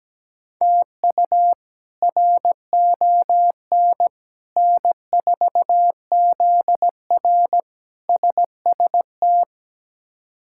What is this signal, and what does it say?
Morse code, heard as a keyed tone, about 17 wpm, 700 Hz. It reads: TU RON N4ZR SST
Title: K1USN SST QSO Practice - 17wpm